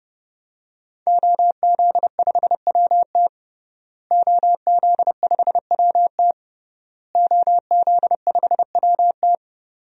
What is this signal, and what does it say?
Morse code, heard as a keyed tone, about 30 wpm, 700 Hz. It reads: OZ5WT OZ5WT OZ5WT
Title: DXCC-100 Call Signs - 5 Characters - Repeated 3x - 30wpm